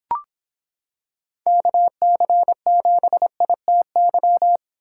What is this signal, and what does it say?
Morse code, heard as a keyed tone, about 26 wpm, 700 Hz. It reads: KC7ITY